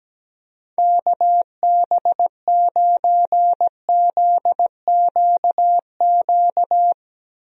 Morse code, heard as a keyed tone, about 17 wpm, 700 Hz. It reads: KB9ZQQ
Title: USA Call Signs - 6 Characters - Repeated 2x - 17wpm